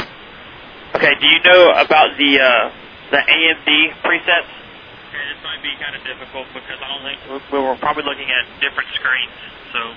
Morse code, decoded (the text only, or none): none